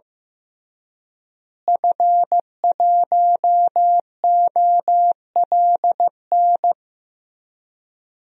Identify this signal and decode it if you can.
Morse code, heard as a keyed tone, about 15 wpm, 700 Hz. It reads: F1OLN
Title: DXCC-100 Call Signs - 5 Characters - Repeated 3x - 15wpm